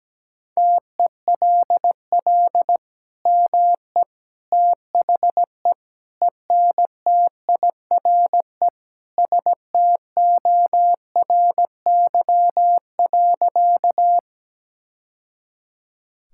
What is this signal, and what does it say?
Morse code, heard as a keyed tone, about 17 wpm, 700 Hz. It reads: TELL ME THE ENTIRE STORY.
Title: Sentences from Top 1000 Words - Review Part 5 - 17wpm